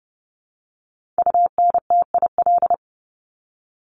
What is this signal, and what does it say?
Morse code, heard as a keyed tone, about 30 wpm, 700 Hz. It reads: UNTIL
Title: Top 500 Words 30wpm